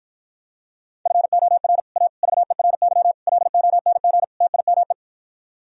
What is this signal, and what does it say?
Morse code, heard as a keyed tone, about 52 wpm, 700 Hz. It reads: FOR A VERY LONG TIME